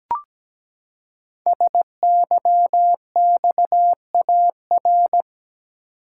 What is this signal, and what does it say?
Morse code, heard as a keyed tone, about 17 wpm, 700 Hz. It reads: SYXAR
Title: Code Groups - 5 Letters - 17wpm